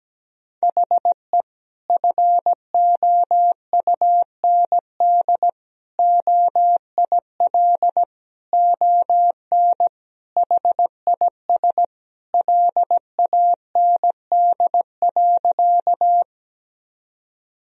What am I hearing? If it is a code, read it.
Morse code, heard as a keyed tone, about 17 wpm, 700 Hz. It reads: HE FOUND OIL ON HIS LAND.
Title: Sentences from Top 1000 Words - Review Part 4 - 17wpm